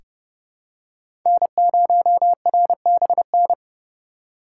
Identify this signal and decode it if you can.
Morse code, heard as a keyed tone, about 30 wpm, 700 Hz. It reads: N0RBN
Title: USA Call Signs - 5 Characters - 2x Character Spacing - 30wpm